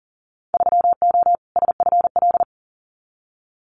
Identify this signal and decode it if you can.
Morse code, heard as a keyed tone, about 41 wpm, 700 Hz. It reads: 3O SFL
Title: Winter Field Day Exchanges 40wpm